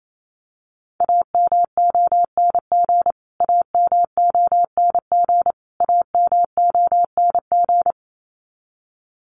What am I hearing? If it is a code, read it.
Morse code, heard as a keyed tone, about 28 wpm, 700 Hz. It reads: AMONG AMONG AMONG
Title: Top 500 Words Repeated 28wpm